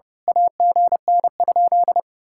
Morse code, heard as a keyed tone, about 33 wpm, 700 Hz. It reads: AGN?
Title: Speed-Racing Common QSO Phrases - 30wpm